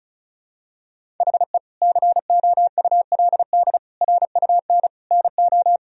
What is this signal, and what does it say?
Morse code, heard as a keyed tone, about 36 wpm, 700 Hz. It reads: HE COULD RUN NO
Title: Sentences from Top 1000 Words - Review Part 1 - 35wpm